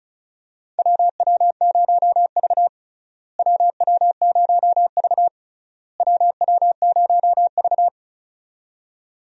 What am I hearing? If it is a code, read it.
Morse code, heard as a keyed tone, about 36 wpm, 700 Hz. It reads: WW0V WW0V WW0V
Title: USA Call Signs - 4 Characters - Repeated 3x - 35wpm